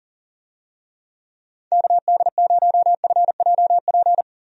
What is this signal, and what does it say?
Morse code, heard as a keyed tone, about 40 wpm, 700 Hz. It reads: KD0FJP
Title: USA Call Signs - 6 Characters - 40wpm